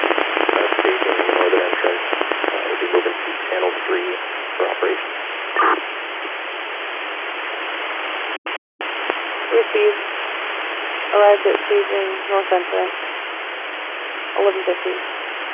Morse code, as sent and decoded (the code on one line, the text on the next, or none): none